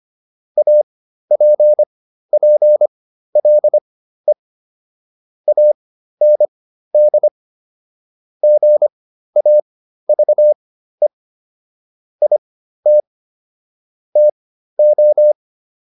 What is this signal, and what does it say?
Morse code, heard as a keyed tone, about 25 wpm, spaced as for 13 wpm, 600 Hz. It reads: APPLE AND GAVE IT TO